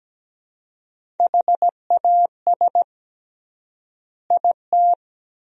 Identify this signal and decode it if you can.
Morse code, heard as a keyed tone, about 17 wpm, 700 Hz. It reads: HAS IT
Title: 3x Word-Spacing - Sentences from Top 100 Words - 17wpm